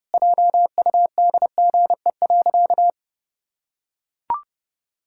Morse code, heard as a keyed tone, about 30 wpm, 700 Hz. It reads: JUDGE.
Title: Pangram Challenge v2 - 30wpm